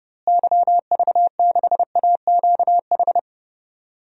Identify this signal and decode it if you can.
Morse code, heard as a keyed tone, about 30 wpm, 700 Hz. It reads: YV6AQH
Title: DXCC-100 Call Signs - 6 Characters - Repeated 2x - 30wpm